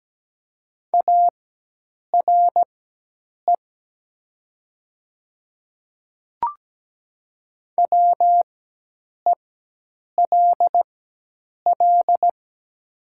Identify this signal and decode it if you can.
Morse code, heard as a keyed tone, about 17 wpm, spaced as for 8 wpm, 700 Hz. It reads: ARE WELL
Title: 4x Character Spacing - Top 200 Words - 17wpm